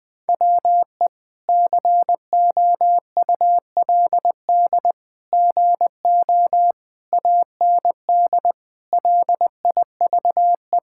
Morse code, heard as a keyed tone, about 20 wpm, 700 Hz. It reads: WE COULD GO AND LIVE
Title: Sentences from Top 1600 Words 20wpm